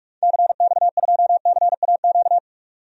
Morse code, heard as a keyed tone, about 45 wpm, 700 Hz. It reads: CX2CAQ
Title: DXCC-100 Call Signs - 6 Characters - 45wpm